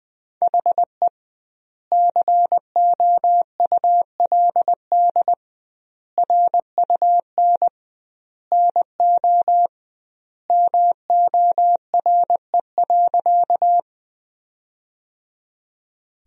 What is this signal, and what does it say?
Morse code, heard as a keyed tone, about 20 wpm, 700 Hz. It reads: HE COULD RUN NO MORE.
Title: 2x Word-Spacing - Sentences from Top 300 Words - 20wpm